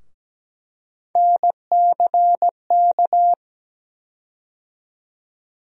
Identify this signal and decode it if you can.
Morse code, heard as a keyed tone, about 17 wpm, 700 Hz. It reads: NCK
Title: USA Call Sign Postfixes - 3 Letters 17wpm